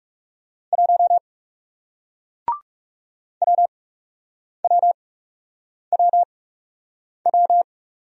Morse code, heard as a keyed tone, about 40 wpm, spaced as for 8 wpm, 700 Hz. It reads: 1 WWWW